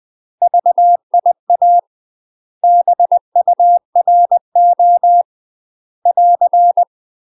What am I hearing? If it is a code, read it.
Morse code, heard as a keyed tone, about 20 wpm, 700 Hz. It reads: VIA BURO <AR>
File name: UK MASTER QSOs TRAINING / 20WPM / UK MASTER QSOs 0063